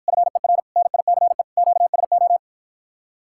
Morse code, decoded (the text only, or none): FER NICE QSO